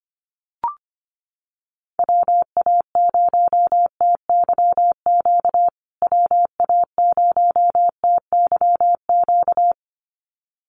Morse code, heard as a keyed tone, about 25 wpm, 700 Hz. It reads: WA0TYQ WA0TYQ
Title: USA Call Signs Repeated 2x 25wpm